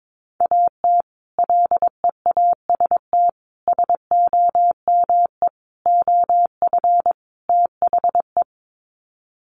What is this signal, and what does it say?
Morse code, heard as a keyed tone, about 22 wpm, 700 Hz. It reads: AT LEAST SOME OF THE
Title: Sets of 5 Words Encore 22wpm